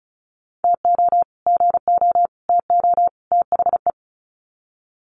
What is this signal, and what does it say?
Morse code, heard as a keyed tone, about 34 wpm, 700 Hz. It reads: TO GO TO THE
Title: Sets of 4 Words 35wpm